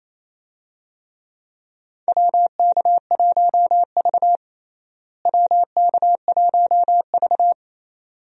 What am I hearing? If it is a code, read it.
Morse code, heard as a keyed tone, about 28 wpm, 700 Hz. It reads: WK1V WK1V
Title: USA Call Signs - 4 Characters - Repeated 2x - 28wpm